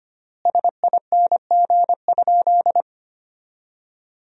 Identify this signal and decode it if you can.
Morse code, heard as a keyed tone, about 25 wpm, 700 Hz. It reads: SING?
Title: 3x Word-Spacing - Sentences from Top 400 Words - 25wpm